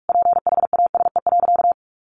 Morse code, heard as a keyed tone, about 44 wpm, 700 Hz. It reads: PHASE.
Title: Sentences from Top 1200 Words 45wpm